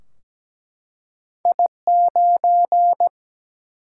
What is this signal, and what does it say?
Morse code, heard as a keyed tone, about 17 wpm, 700 Hz. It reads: I9
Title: Single-Letter Single-Number 17wpm